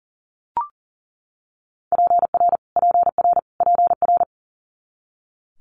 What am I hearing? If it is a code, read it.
Morse code, heard as a keyed tone, about 42 wpm, 700 Hz. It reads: PR PR PR
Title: Multi-letter Phonemes Repeated 3x 40wpm